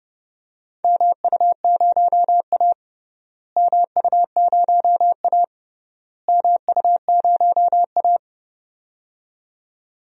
Morse code, heard as a keyed tone, about 30 wpm, 700 Hz. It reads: MU0A MU0A MU0A
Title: DXCC-100 Call Signs - 4 Characters - Repeated 3x - 30wpm